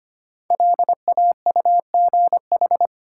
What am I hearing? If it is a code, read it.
Morse code, heard as a keyed tone, about 25 wpm, 700 Hz. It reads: LAUGH
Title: Sentences from Top 500 Words - New Word Drill - 25wpm